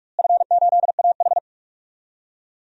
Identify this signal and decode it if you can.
Morse code, heard as a keyed tone, about 46 wpm, 700 Hz. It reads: F8AH
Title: DXCC-100 Call Signs - 4 Characters - Repeated 3x - 45wpm